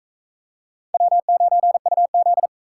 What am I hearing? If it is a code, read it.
Morse code, heard as a keyed tone, about 40 wpm, 700 Hz. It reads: W9UZ